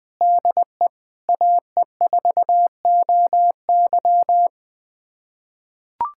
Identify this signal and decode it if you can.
Morse code, heard as a keyed tone, about 20 wpm, 700 Hz. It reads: DE AE4OY
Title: USA Callsign DE Callsign 20wpm